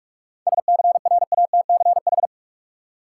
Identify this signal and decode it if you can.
Morse code, heard as a keyed tone, about 46 wpm, 700 Hz. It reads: SCRATCH
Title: Seven Letter Words - 45wpm